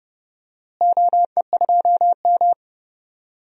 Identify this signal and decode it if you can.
Morse code, heard as a keyed tone, about 32 wpm, 700 Hz. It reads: OE2M
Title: DXCC-100 Call Signs - 4 Characters - Repeated 3x - 30wpm